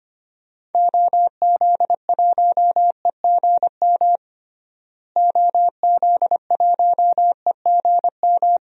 Morse code, heard as a keyed tone, about 25 wpm, 700 Hz. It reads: OZ1EGM OZ1EGM
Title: DXCC-100 Call Signs - 6 Characters - Repeated 3x - 25wpm